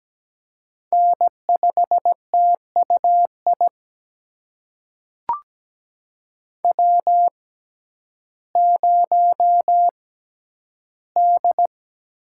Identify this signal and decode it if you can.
Morse code, heard as a keyed tone, about 17 wpm, spaced as for 10 wpm, 700 Hz. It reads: N5TUI W0D
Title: USA Call Signs - 5 Characters - 6x Character Spacing - 17wpm